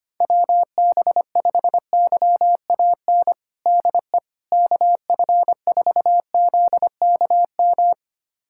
Morse code, heard as a keyed tone, about 25 wpm, 700 Hz. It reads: WB5YAN DE KF4ZKM
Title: USA Callsign DE Callsign 25wpm